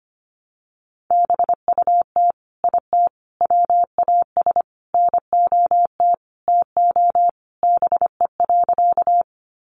Morse code, heard as a keyed tone, about 25 wpm, 700 Hz. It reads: BUT IT WAS NOT TO BE.
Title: Sentences from Top 600 Words 25wpm